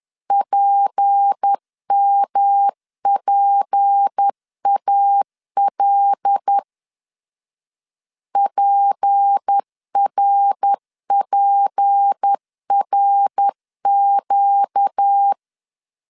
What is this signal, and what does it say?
Morse code, heard as a keyed tone, about 10 wpm, 800 Hz. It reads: PMPAL PRPRQ
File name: LEZIONI / 6 PKX0 / 6 P 10wpm 2432